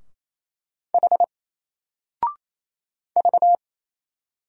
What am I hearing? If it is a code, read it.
Morse code, heard as a keyed tone, about 28 wpm, 700 Hz. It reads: H V
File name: Single-Letter-Number-28wpm